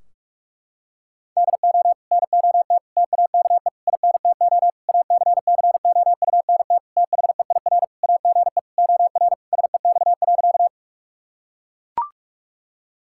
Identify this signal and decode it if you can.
Morse code, heard as a keyed tone, about 41 wpm, 700 Hz. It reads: DO NOT TAKE INTO ACCOUNT THEIR AGE OR SEX.
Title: Sentences from Top 1600 Words 45wpm